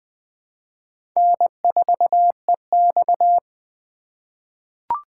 Morse code, heard as a keyed tone, about 20 wpm, 700 Hz. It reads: N4EX
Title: 2023 POTA Hunter Call Signs - 20wpm